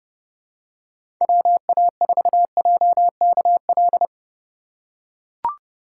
Morse code, heard as a keyed tone, about 30 wpm, 700 Hz. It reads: WA4JKL
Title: K1USN SST QSO Practice - 30wpm